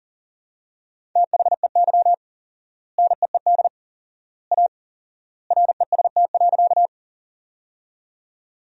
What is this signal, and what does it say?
Morse code, heard as a keyed tone, about 40 wpm, 700 Hz. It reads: THEY NEED A REST.
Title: 4x Word-Spacing - Sentences from Top 500 Words - 40wpm